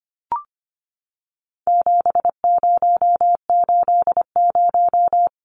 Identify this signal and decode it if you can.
Morse code, heard as a keyed tone, about 25 wpm, 700 Hz. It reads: 7080
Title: Numbers - 4-Digit 25wpm